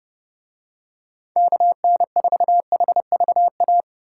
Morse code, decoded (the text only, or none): KN4HVA